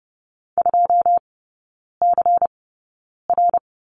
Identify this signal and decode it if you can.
Morse code, heard as a keyed tone, about 30 wpm, spaced as for 9 wpm, 700 Hz. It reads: 2CR
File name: Lesson 17e - Letter u - Three Characters 30wpm